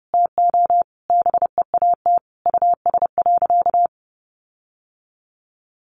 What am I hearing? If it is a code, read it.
Morse code, heard as a keyed tone, about 30 wpm, 700 Hz. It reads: TO BEAT US.
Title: Sentences from Top 1300 Words 30wpm